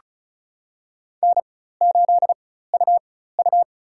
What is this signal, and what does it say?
Morse code, heard as a keyed tone, about 34 wpm, spaced as for 16 wpm, 700 Hz. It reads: N8UU
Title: USA Call Signs - 4 Characters - 4x Character Spacing - 35wpm